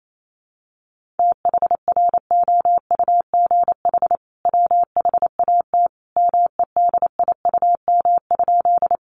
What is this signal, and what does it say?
Morse code, heard as a keyed tone, about 28 wpm, 700 Hz. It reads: THROUGH WHAT MEDIUM?
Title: Sentences from Top 900 Words 28wpm